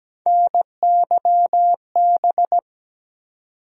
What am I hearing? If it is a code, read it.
Morse code, heard as a keyed tone, about 17 wpm, 700 Hz. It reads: NYB
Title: USA Call Sign Postfixes - 3 Letters 17wpm